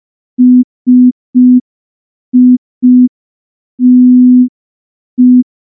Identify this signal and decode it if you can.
Morse code, heard as a keyed tone, about 5 wpm, 250 Hz. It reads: SITE
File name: morse / morse-training-05WPM